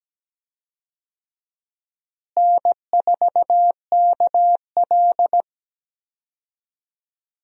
Morse code, heard as a keyed tone, about 17 wpm, 700 Hz. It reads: N4KL
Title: USA Call Signs - 4 Characters - Repeated 2x - 17wpm